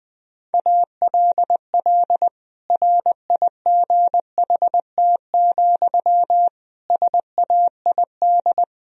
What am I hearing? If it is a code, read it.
Morse code, heard as a keyed tone, about 21 wpm, 700 Hz. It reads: ALL RIGHT, SAID
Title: Sentences from Top 1000 Words - Review Part 4 - 20wpm